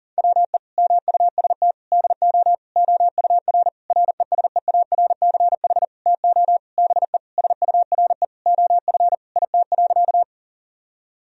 Morse code, heard as a keyed tone, about 37 wpm, 700 Hz. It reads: WE MUST DO OUR RESEARCH TO BE SURE OF IT.
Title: Sentences from Top 1000 Words - Review Part 4 - 40wpm